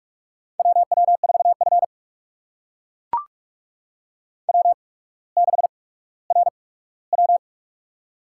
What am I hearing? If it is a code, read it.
Morse code, heard as a keyed tone, about 41 wpm, spaced as for 19 wpm, 700 Hz. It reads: WW4F W6RW
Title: USA Call Signs - 4 Characters - 8x Character Spacing - 45wpm